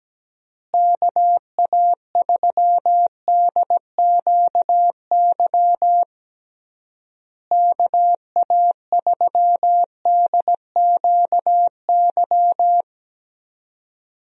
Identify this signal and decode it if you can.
Morse code, heard as a keyed tone, about 17 wpm, 700 Hz. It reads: KA3DQY KA3DQY
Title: USA Call Signs - 6 Characters - Repeated 3x - 17wpm